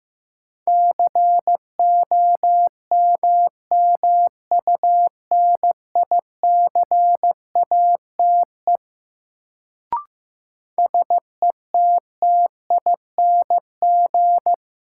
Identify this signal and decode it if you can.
Morse code, heard as a keyed tone, about 15 wpm, 700 Hz. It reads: COMMUNICATE SETTING
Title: Sentences from Top 1300 Words - New Word Drill - 15wpm